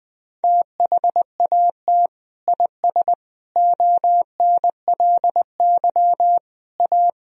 Morse code, heard as a keyed tone, about 20 wpm, 700 Hz. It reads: THAT IS ONLY A
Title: Sentences from Top 1000 Words - Review Part 4 - 20wpm